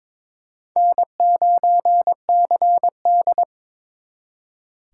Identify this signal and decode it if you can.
Morse code, heard as a keyed tone, about 30 wpm, spaced as for 26 wpm, 700 Hz. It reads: N9CD
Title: Speed Racing - USA 1x2 Call Signs 22wpm